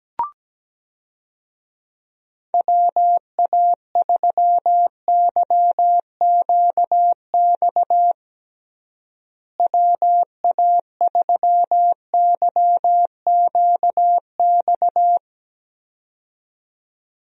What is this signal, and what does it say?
Morse code, heard as a keyed tone, about 17 wpm, 700 Hz. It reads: WA3YQX WA3YQX
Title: USA Call Signs - 6 Characters - Repeated 2x - 17wpm